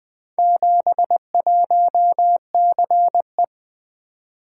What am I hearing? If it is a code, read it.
Morse code, heard as a keyed tone, about 21 wpm, 700 Hz. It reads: 71CE
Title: Four Letter-Number Combinations - 20wpm